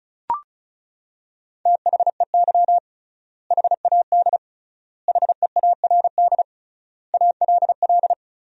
Morse code, heard as a keyed tone, about 36 wpm, 700 Hz. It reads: THEY HAD HEARD ALL
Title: 3x Word-Spacing - Sentences from Top 400 Words - 35wpm